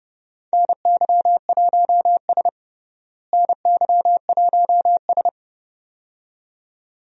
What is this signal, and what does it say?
Morse code, heard as a keyed tone, about 30 wpm, 700 Hz. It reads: NY1S NY1S
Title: USA Call Signs - 4 Characters - Repeated 2x - 30wpm